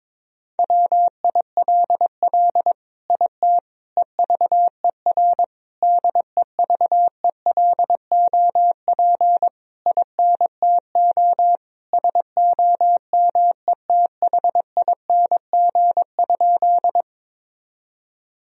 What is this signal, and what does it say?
Morse code, heard as a keyed tone, about 21 wpm, 700 Hz. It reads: WILL IT EVER DEVELOP INTO SOMETHING?
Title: Sentences from Top 1000 Words - Review Part 2 - 22wpm